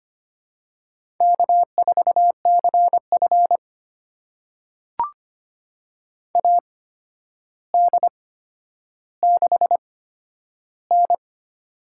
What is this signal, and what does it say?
Morse code, heard as a keyed tone, about 25 wpm, spaced as for 11 wpm, 700 Hz. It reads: K4CF AD6N
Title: USA Call Signs - 4 Characters - 8x Character Spacing - 25wpm